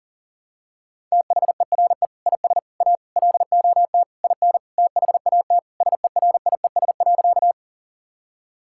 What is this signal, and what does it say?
Morse code, heard as a keyed tone, about 37 wpm, 700 Hz. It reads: THERE IS A LOT IN THAT SERIES.
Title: Sentences from Top 1000 Words - Review Part 5 - 40wpm